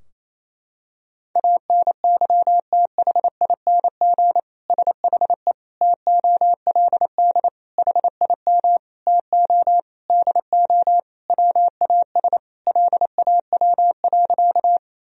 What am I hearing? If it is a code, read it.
Morse code, heard as a keyed tone, about 28 wpm, 700 Hz. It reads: ANYTHING SHE TOLD HIM TO DO WAS LAW.
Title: Sentences from Top 1000 Words - Review Part 3 - 28wpm